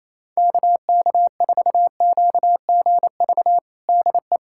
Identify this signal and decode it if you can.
Morse code, heard as a keyed tone, about 28 wpm, 700 Hz. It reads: KK4QGV DE
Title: USA Callsign DE Callsign 28wpm